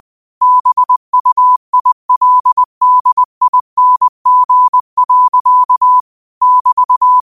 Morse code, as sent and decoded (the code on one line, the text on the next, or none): -... ..- .. .-.. -.. .. -. --. .-.-.- / -...-
BUILDING. <BT>